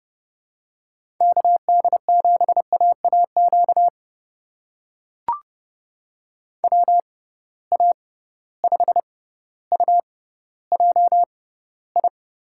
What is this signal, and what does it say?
Morse code, heard as a keyed tone, about 30 wpm, spaced as for 15 wpm, 700 Hz. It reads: KD7AAQ WA5UJI
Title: USA Call Signs - 6 Characters - 6x Character Spacing - 30wpm